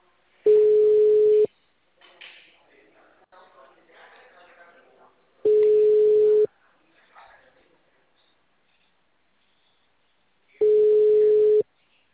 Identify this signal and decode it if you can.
Ringback tone (Europe, 425 Hz), with speech.